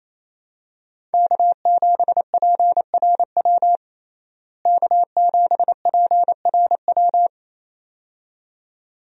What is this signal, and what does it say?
Morse code, heard as a keyed tone, about 28 wpm, 700 Hz. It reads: K7PRW K7PRW
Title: USA Call Signs - 5 Characters - Repeated 2x - 28wpm